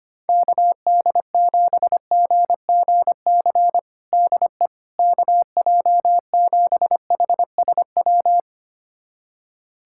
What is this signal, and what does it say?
Morse code, heard as a keyed tone, about 25 wpm, 700 Hz. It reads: KD7GGC DE KJ7HSW
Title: USA Callsign DE Callsign - 6 Characters - 25wpm